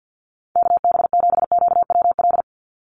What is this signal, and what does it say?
Morse code, heard as a keyed tone, about 50 wpm, 700 Hz. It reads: KB7QWL
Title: USA Call Signs 50wpm